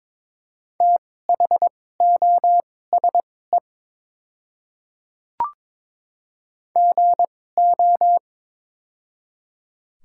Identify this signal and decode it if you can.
Morse code, heard as a keyed tone, about 22 wpm, spaced as for 16 wpm, 700 Hz. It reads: THOSE GO
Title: 2x Character Spacing - Top 300 Words - 22wpm